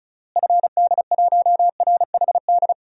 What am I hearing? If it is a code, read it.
Morse code, heard as a keyed tone, about 34 wpm, 700 Hz. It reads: FD1RHD
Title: DXCC-100 Call Signs - 6 Characters - Repeated 2x - 35wpm